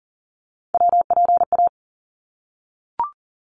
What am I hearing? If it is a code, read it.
Morse code, heard as a keyed tone, about 39 wpm, 700 Hz. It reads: WPA
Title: ARRL Sections 40wpm